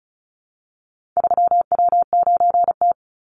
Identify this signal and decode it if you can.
Morse code, heard as a keyed tone, about 36 wpm, 700 Hz. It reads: 3W9T
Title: Call Signs - 35wpm